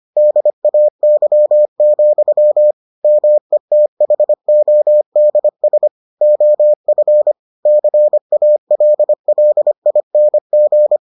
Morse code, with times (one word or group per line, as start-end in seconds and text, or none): 0.2-2.7 DAY,
3.0-5.9 METHODS
6.2-7.3 OF
7.7-11.0 CALLING